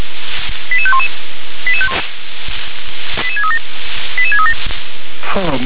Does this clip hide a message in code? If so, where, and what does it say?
Yes: five-tone selective call (ZVEI) from 0.7 to 4.5 s, heard as quick runs of five tones, tones of about 80 ms, with speech. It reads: 88511 88637 88637